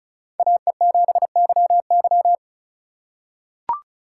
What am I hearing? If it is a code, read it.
Morse code, heard as a keyed tone, about 36 wpm, 700 Hz. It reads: AE7YY
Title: USA Call Signs - 5 Characters - 8x Character Spacing - 35wpm